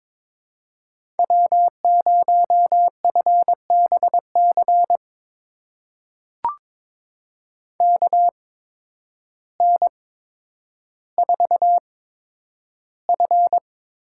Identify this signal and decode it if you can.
Morse code, heard as a keyed tone, about 22 wpm, spaced as for 10 wpm, 700 Hz. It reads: W0FBC KN4F
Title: USA Call Signs - 5 Characters - 8x Character Spacing - 22wpm